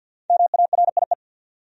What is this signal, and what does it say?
Morse code, heard as a keyed tone, about 47 wpm, 700 Hz. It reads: MARIE
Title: USA Common Names - Max 5 Letters - 50wpm